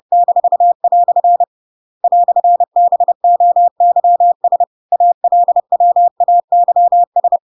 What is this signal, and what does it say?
Morse code, heard as a keyed tone, about 30 wpm, 700 Hz. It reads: -" "BOYS ALWAYS
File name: Pinocchio / Pinocchio ch8 30 30 ik1hge